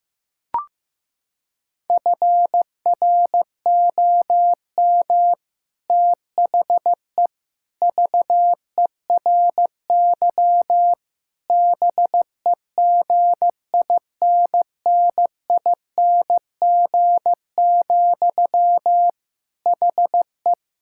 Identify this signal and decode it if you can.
Morse code, heard as a keyed tone, about 16 wpm, 700 Hz. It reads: FROM THE VERY BEGINNING, HE
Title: Sentences from Top 1600 Words 15wpm